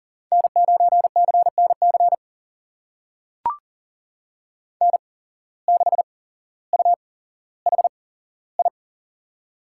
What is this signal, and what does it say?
Morse code, heard as a keyed tone, about 40 wpm, spaced as for 17 wpm, 700 Hz. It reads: N9CNC N6UHI
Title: USA Call Signs - 5 Characters - 8x Character Spacing - 40wpm